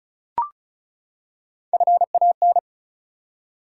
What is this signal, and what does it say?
Morse code, heard as a keyed tone, about 34 wpm, 700 Hz. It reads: FAN
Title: Three Letter Words - 35wpm